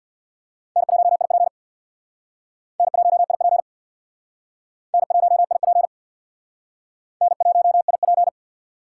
Morse code, heard as a keyed tone, about 66 wpm, 700 Hz. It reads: N1IP N1IP N1IP N1IP